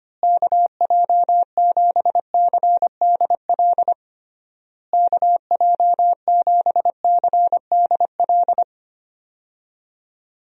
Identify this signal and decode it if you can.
Morse code, heard as a keyed tone, about 25 wpm, 700 Hz. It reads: KJ7CDL KJ7CDL
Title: USA Call Signs - 6 Characters - Repeated 3x - 25wpm